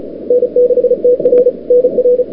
Morse code, heard as a keyed tone, about 36 wpm, 440 Hz. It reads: N6TR NR
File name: n6tr / Fab2007CWSprintHour1